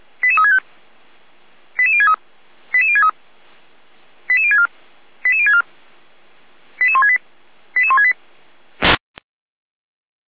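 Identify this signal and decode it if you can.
Five-tone selective call (ZVEI), heard as quick runs of five tones, tones of about 70 ms. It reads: 88365 88063 88063 88064 88064 88168 88168
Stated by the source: ZVEI five-tone selective call received by SDR.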